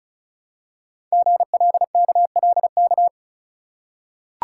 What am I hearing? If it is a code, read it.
Morse code, heard as a keyed tone, about 36 wpm, 700 Hz. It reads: GLKLK
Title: Code Groups - 5 Letters - 35wpm